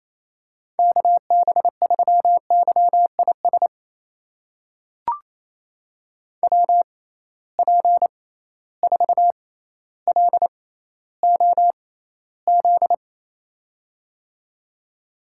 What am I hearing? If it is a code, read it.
Morse code, heard as a keyed tone, about 28 wpm, spaced as for 14 wpm, 700 Hz. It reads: KB3YIS WP4LOZ
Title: USA Call Signs - 6 Characters - 6x Character Spacing - 28wpm